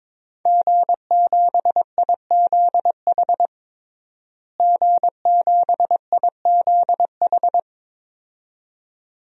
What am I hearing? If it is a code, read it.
Morse code, heard as a keyed tone, about 22 wpm, 700 Hz. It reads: G7IZH G7IZH
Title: DXCC-100 Call Signs - 5 Characters - Repeated 2x - 22wpm